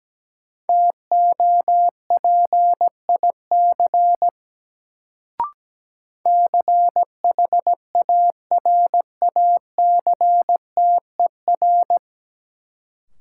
Morse code, heard as a keyed tone, about 17 wpm, 700 Hz. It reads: TOPIC CHARACTER
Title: Sentences from Top 1200 Words - New Word Drill - 17wpm